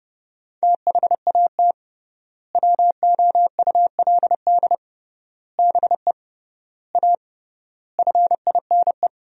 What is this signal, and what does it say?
Morse code, heard as a keyed tone, about 30 wpm, 700 Hz. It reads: THAT WOULD BE A FINE
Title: 3x Word-Spacing - Sentences from Top 500 Words - 30wpm